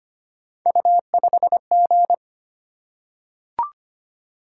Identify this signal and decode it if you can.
Morse code, heard as a keyed tone, about 25 wpm, 700 Hz. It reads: U5G
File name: Three Letter-Number Combinations - 25wpm